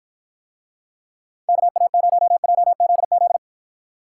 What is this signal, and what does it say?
Morse code, heard as a keyed tone, about 53 wpm, 700 Hz. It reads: KA0JZZ